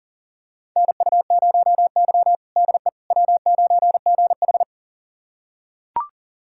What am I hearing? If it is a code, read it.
Morse code, heard as a keyed tone, about 40 wpm, 700 Hz. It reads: NU0Y DE W9GH
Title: USA Callsign DE Callsign - 4 Characters - 40wpm